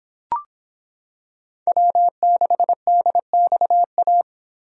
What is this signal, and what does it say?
Morse code, heard as a keyed tone, about 26 wpm, 700 Hz. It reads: W6DXA